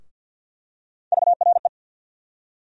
Morse code, heard as a keyed tone, about 52 wpm, 700 Hz. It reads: VRE
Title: USA Call Sign Postfixes - 3 Letters 50wpm